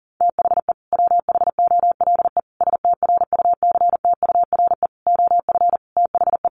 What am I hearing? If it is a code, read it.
Morse code, heard as a keyed tone, about 40 wpm, 700 Hz. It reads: THE WHOLE STRUCTURE OF THE
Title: Sentences from Top 1500 Words 40wpm